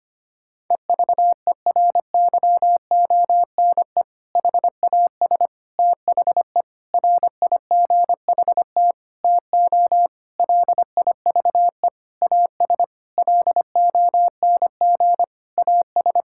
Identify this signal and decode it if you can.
Morse code, heard as a keyed tone, about 25 wpm, 700 Hz. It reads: EVERYONE HAS THE RIGHT TO LIVE AS LONG AS
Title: Sentences from Top 1000 Words - Review Part 5 - 25wpm